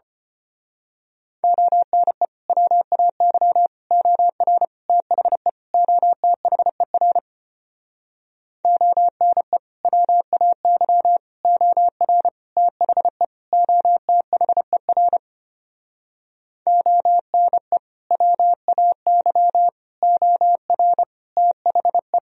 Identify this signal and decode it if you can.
Morse code, heard as a keyed tone, about 31 wpm, 700 Hz. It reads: ONE WAY OR THE OTHER ONE WAY OR THE OTHER ONE WAY OR THE
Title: Speed-Racing - Sets of 5 Words 25wpm